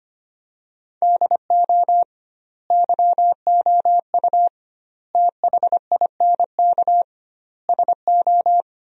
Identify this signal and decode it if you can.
Morse code, heard as a keyed tone, about 25 wpm, 700 Hz. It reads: DO YOU THINK SO
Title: 2x Word-Spacing - Sentences from Top 200 Words - 25wpm